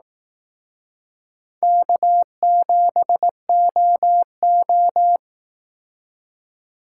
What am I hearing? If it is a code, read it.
Morse code, heard as a keyed tone, about 18 wpm, 700 Hz. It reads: K7OO
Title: Speed Racing - USA 1x2 Call Signs 15wpm